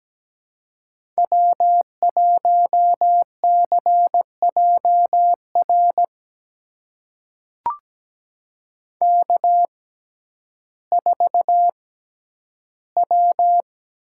Morse code, heard as a keyed tone, about 17 wpm, spaced as for 10 wpm, 700 Hz. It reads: W1CJR K4W
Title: USA Call Signs - 5 Characters - 6x Character Spacing - 17wpm